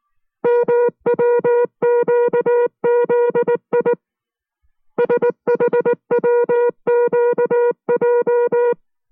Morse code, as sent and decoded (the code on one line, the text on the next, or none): -- .-- --.- --.. .. / ... .... .-- --.- .---
MWQZI SHWQJ